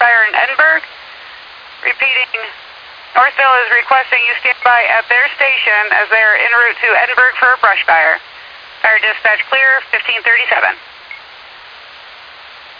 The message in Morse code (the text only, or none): none